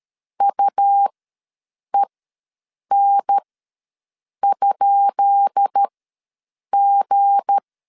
Morse code, heard as a keyed tone, about 13 wpm, spaced as for 7 wpm, 800 Hz. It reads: UEN?G
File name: LEZIONI / 10 SIGNS / 10D COMA ws 12wpm 3618